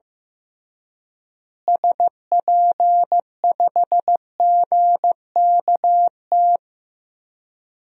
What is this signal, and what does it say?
Morse code, heard as a keyed tone, about 15 wpm, 700 Hz. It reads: SP5GKT
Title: DXCC-100 Call Signs - 6 Characters - Repeated 3x - 15wpm